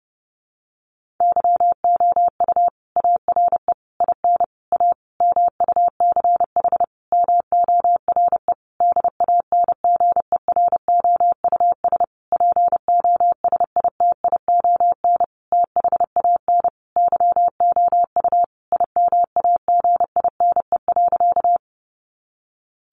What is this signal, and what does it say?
Morse code, heard as a keyed tone, about 30 wpm, 700 Hz. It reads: YOU ARE IN A MUCH MORE DANGEROUS POSITION THAN YOU IMAGINE.
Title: Sentences from Top 1400 Words 30wpm